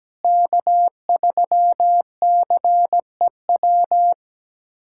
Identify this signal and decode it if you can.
Morse code, heard as a keyed tone, about 17 wpm, 700 Hz. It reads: K3CEW
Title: USA Callsign DE Callsign 17wpm